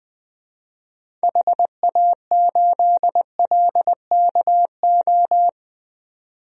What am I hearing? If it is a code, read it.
Morse code, heard as a keyed tone, about 21 wpm, 700 Hz. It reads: HA8LKO
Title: DXCC-100 Call Signs - 6 Characters - 20wpm